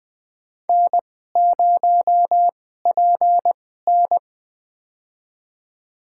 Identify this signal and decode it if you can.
Morse code, heard as a keyed tone, about 20 wpm, spaced as for 14 wpm, 700 Hz. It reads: N0PN
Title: USA Call Signs - 2x Character Spacing - 20wpm